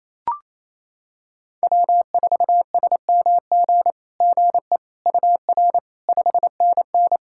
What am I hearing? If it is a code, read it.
Morse code, heard as a keyed tone, about 28 wpm, 700 Hz. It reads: W4SMG GE UR 5NN
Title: SOTA QSO Practice 28wpm